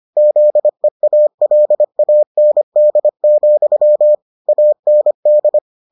Morse code, heard as a keyed tone, about 25 wpm, 600 Hz. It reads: ZEALAND, AND